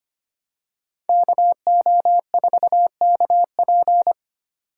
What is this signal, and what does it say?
Morse code, heard as a keyed tone, about 29 wpm, 700 Hz. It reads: KO4KP
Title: Speed Racing - USA 2x2 Call Signs 25wpm